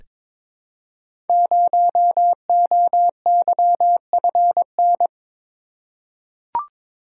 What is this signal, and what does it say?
Morse code, heard as a keyed tone, about 22 wpm, 700 Hz. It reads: 0OYFN
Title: Code Groups - 5 Letter:Numbers - 22wpm